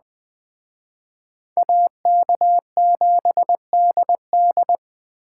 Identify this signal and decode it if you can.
Morse code, heard as a keyed tone, about 21 wpm, 700 Hz. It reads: AK7DD
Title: Speed Racing - USA 2x2 Call Signs 17wpm